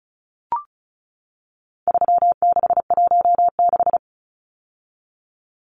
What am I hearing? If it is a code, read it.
Morse code, heard as a keyed tone, about 34 wpm, 700 Hz. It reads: 3616
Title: Numbers - 4-Digit 35wpm